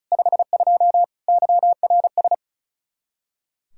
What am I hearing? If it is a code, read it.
Morse code, heard as a keyed tone, about 46 wpm, 700 Hz. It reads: 52 YRS